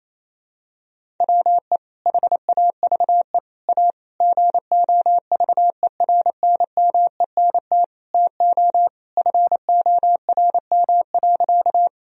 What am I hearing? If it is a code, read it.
Morse code, heard as a keyed tone, about 28 wpm, 700 Hz. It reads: WE HAVE A GOVERNMENT TO FORM.
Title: Sentences from Top 1000 Words - Review Part 1 - 28wpm